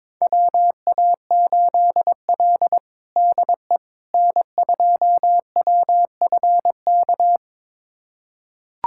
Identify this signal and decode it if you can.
Morse code, heard as a keyed tone, about 22 wpm, 700 Hz. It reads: WA8L DE N2WFK
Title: USA Callsign DE Callsign 22wpm